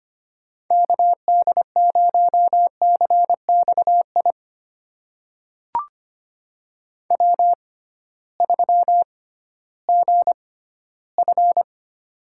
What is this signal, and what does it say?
Morse code, heard as a keyed tone, about 25 wpm, spaced as for 15 wpm, 700 Hz. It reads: KD0CXI W3GF
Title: USA Call Signs - 6x Character Spacing - 25wpm